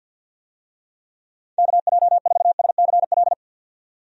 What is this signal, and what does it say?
Morse code, heard as a keyed tone, about 52 wpm, 700 Hz. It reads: KJ4SCL
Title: USA Call Signs - 6 Characters - 50wpm